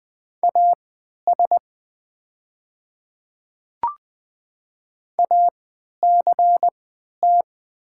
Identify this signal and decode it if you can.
Morse code, heard as a keyed tone, about 20 wpm, spaced as for 11 wpm, 690 Hz. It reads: AS ACT
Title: 3x Character Spacing - Top 500 Words - 20wpm